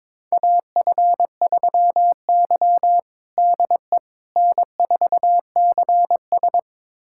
Morse code, heard as a keyed tone, about 22 wpm, 700 Hz. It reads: AF3Y DE N4CS
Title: USA Callsign DE Callsign - 4 Characters - 22wpm